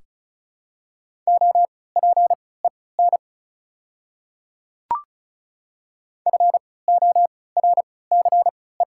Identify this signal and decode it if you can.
Morse code, heard as a keyed tone, about 36 wpm, spaced as for 20 wpm, 700 Hz. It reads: OPEN FORCE
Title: 3x Character Spacing - Top 500 Words - 35wpm